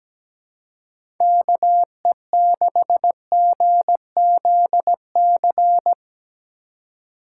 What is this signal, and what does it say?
Morse code, heard as a keyed tone, about 17 wpm, 700 Hz. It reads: KE6GZC
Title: USA Call Signs - 6 Characters - 2x Character Spacing - 17wpm